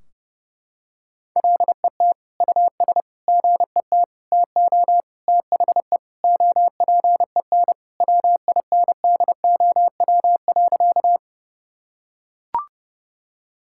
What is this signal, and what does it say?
Morse code, heard as a keyed tone, about 30 wpm, 700 Hz. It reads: LET US GET TO THE OPEN WINDOW.
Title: Sentences from Top 1600 Words 30wpm